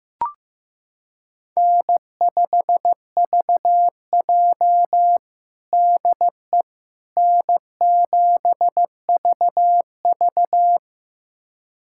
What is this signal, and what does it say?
Morse code, heard as a keyed tone, about 15 wpm, 700 Hz. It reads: N5VJ DE N7VV
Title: USA Callsign DE Callsign - 4 Characters - 15wpm